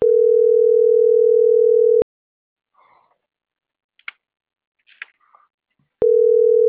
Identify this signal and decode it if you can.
Ringback tone (North America), 440 + 480 Hz.
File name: CENTRAL MAERSK / out luis.herrera